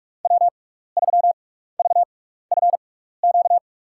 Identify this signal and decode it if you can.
Morse code, heard as a keyed tone, about 44 wpm, spaced as for 15 wpm, 700 Hz. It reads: W3VFQ
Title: USA Call Signs - 6x Character Spacing - 45wpm